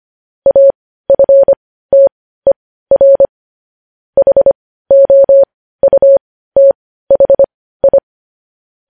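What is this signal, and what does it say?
Morse code, heard as a keyed tone, about 25 wpm, spaced as for 15 wpm, 550 Hz. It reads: AFTER HOUTHI
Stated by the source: Morse code news headlines.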